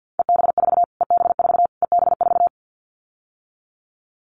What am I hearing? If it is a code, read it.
Morse code, heard as a keyed tone, about 50 wpm, 700 Hz. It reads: EB4 EB4 EB4
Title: Callsign Prefixes for Top 25 DXCC - Repeated 3x - 50wpm